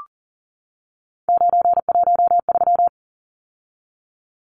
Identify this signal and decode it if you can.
Morse code, heard as a keyed tone, about 40 wpm, 700 Hz. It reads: 913
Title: Numbers - 3-Digit 40wpm